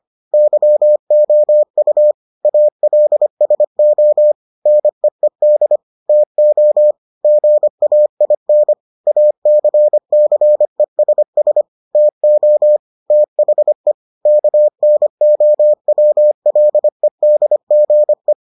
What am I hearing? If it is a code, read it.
Morse code, heard as a keyed tone, about 25 wpm, 600 Hz. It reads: YOU ALSO NEED TO GAIN ACCESS TO THE KNOWLEDGE